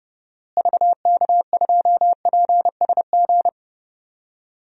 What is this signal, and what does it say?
Morse code, heard as a keyed tone, about 29 wpm, 700 Hz. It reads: VK2PSG
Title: DXCC-100 Call Signs - 6 Characters - Repeated 2x - 30wpm